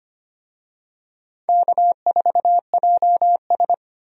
Morse code, heard as a keyed tone, about 25 wpm, 700 Hz. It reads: K4JS
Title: USA Call Signs - 4 Characters - 25wpm